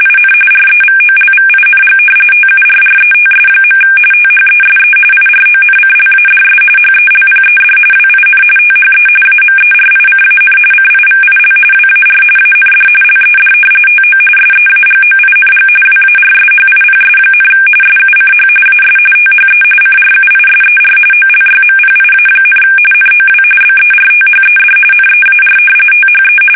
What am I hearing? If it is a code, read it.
RTTY, about 45.45 baud. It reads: IN REVOLVING TURRETS. THESE IRONCLADS HELPED WIN THE AMERICAN CIVIL WAR AND LED TO SEAGOING WARSHIPS, SUCH AS THE CRUISER OLYMPIA, NOW BERTHED NEAR THE
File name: fsk850 / battleshipnj